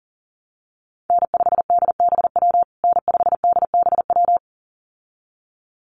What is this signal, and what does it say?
Morse code, heard as a keyed tone, about 40 wpm, 700 Hz. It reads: N5DBW N5DBW
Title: USA Call Signs Repeated 2x 40wpm